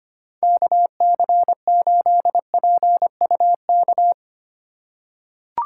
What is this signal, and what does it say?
Morse code, heard as a keyed tone, about 25 wpm, 700 Hz. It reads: KC8PUK
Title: USA Call Signs - 6 Characters - 3x Character Spacing - 25wpm